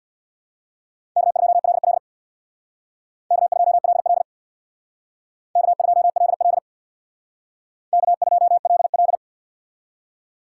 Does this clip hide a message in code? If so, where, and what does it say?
Yes: Morse code from 1.2 to 9.2 s, heard as a keyed tone, about 64 wpm, 700 Hz. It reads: K2LL K2LL K2LL K2LL